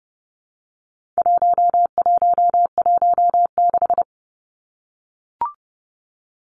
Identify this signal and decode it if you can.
Morse code, heard as a keyed tone, about 30 wpm, 700 Hz. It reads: 1116
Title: Numbers - 4-Digit 30wpm